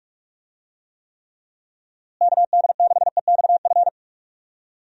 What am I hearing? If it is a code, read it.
Morse code, heard as a keyed tone, about 46 wpm, 700 Hz. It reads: KD6EXF
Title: USA Call Signs - 6 Characters - Repeated 3x - 45wpm